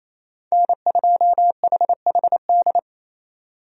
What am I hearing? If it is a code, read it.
Morse code, heard as a keyed tone, about 27 wpm, 700 Hz. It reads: N2HHD
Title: USA Call Signs - 5 Characters - Repeated 2x - 28wpm